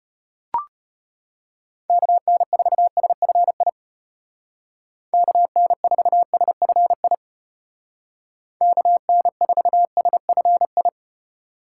Morse code, heard as a keyed tone, about 32 wpm, 700 Hz. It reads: KN4SFI KN4SFI KN4SFI